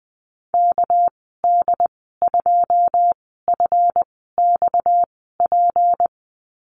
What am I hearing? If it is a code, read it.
Morse code, heard as a keyed tone, about 20 wpm, spaced as for 15 wpm, 700 Hz. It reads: KD2FXP
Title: 231 Lesson 37g - Letter x - Call Signs 20wpm